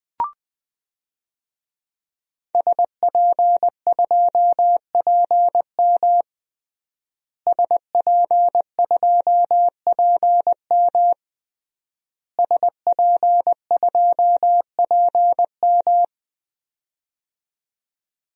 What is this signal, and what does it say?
Morse code, heard as a keyed tone, about 20 wpm, 700 Hz. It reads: SP2PM SP2PM SP2PM
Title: DXCC-100 Call Signs - 5 Characters - Repeated 3x - 20wpm